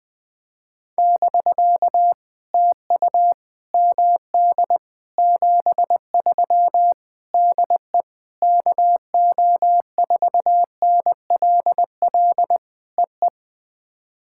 Morse code, heard as a keyed tone, about 20 wpm, 700 Hz. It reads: <BK> TU MD 73 DE KO4NLL EE
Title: POTA QSO Practice - 20wpm